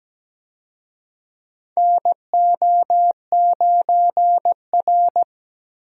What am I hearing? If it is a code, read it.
Morse code, heard as a keyed tone, about 17 wpm, 700 Hz. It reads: NO9R
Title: USA Call Signs - 4 Characters - Repeated 2x - 17wpm